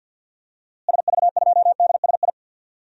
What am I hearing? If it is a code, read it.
Morse code, heard as a keyed tone, about 46 wpm, 700 Hz. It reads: SV2DSI
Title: DXCC-100 Call Signs - 6 Characters - Repeated 3x - 50wpm